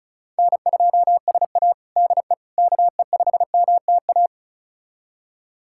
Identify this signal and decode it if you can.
Morse code, heard as a keyed tone, about 34 wpm, 700 Hz. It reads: N2SA DE KE5MTA
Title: USA Callsign DE Callsign 35wpm